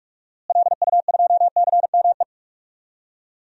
Morse code, heard as a keyed tone, about 41 wpm, 700 Hz. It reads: LU2CME
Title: DXCC-100 Call Signs - 6 Characters - Repeated 3x - 45wpm